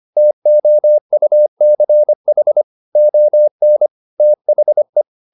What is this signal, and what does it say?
Morse code, heard as a keyed tone, about 25 wpm, 600 Hz. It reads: TOUCH ON THE
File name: the old man and the sea part 1 25wpm